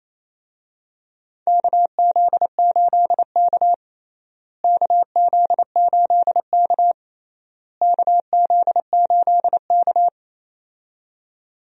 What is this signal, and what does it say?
Morse code, heard as a keyed tone, about 28 wpm, 700 Hz. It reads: KZ8K KZ8K KZ8K
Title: USA Call Signs - 4 Characters - Repeated 3x - 28wpm